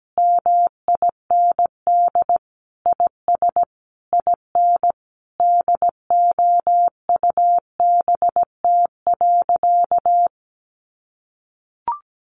Morse code, heard as a keyed tone, about 17 wpm, 700 Hz. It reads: MIND IS IN DOUBT.
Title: Sentences from Top 1500 Words 17wpm